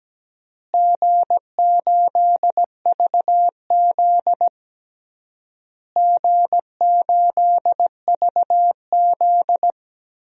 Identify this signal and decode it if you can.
Morse code, heard as a keyed tone, about 17 wpm, 700 Hz. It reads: G8VZ G8VZ
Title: DXCC-100 Call Signs - 4 Characters - Repeated 3x - 17wpm